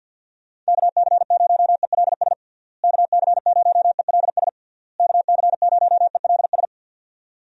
Morse code, heard as a keyed tone, about 52 wpm, 700 Hz. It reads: KC0ELS KC0ELS KC0ELS
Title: USA Call Signs - 6 Characters - Repeated 3x - 50wpm